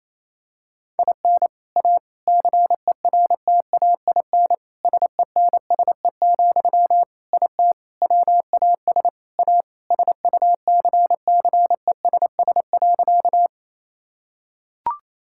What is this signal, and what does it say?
Morse code, heard as a keyed tone, about 28 wpm, 700 Hz. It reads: IN A CERTAIN SENSE, IT WAS A SUCCESS.
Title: Sentences from Top 1000 Words - Review Part 6 - 28wpm